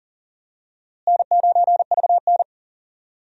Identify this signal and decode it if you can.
Morse code, heard as a keyed tone, about 40 wpm, 700 Hz. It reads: N9VN
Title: USA Call Signs - 4 Characters - Repeated 3x - 40wpm